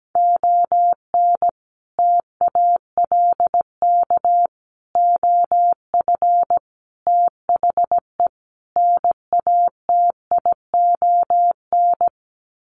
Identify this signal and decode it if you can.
Morse code, heard as a keyed tone, about 17 wpm, 700 Hz. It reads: ON TALK OF THE NATION
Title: Sets of 5 Words 17wpm